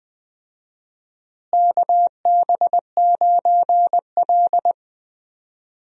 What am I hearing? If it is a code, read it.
Morse code, heard as a keyed tone, about 20 wpm, 700 Hz. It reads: KB9L
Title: USA Call Signs - 4 Characters - Repeated 2x - 20wpm